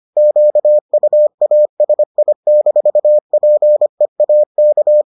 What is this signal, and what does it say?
Morse code, heard as a keyed tone, about 25 wpm, 600 Hz. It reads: QUASI-PEAK